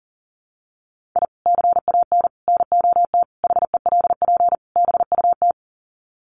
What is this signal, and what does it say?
Morse code, heard as a keyed tone, about 40 wpm, 700 Hz. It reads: I CAN NOT HELP BUT
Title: Sets of 5 Words Encore2 40wpm